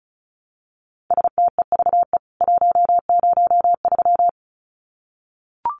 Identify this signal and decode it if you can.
Morse code, heard as a keyed tone, about 34 wpm, 700 Hz. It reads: STEVE 103
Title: CWT Exchange 35wpm